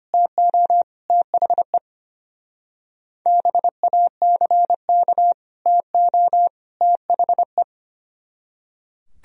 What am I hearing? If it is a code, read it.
Morse code, heard as a keyed tone, about 31 wpm, 700 Hz. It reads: TO THE BACK TO THE
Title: Speed-Racing - Sets of 3 Words 25wpm